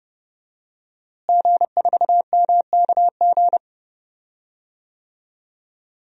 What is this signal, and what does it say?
Morse code, heard as a keyed tone, about 32 wpm, 700 Hz. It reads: G4MKG
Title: DXCC-100 Call Signs - 5 Characters - Repeated 3x - 30wpm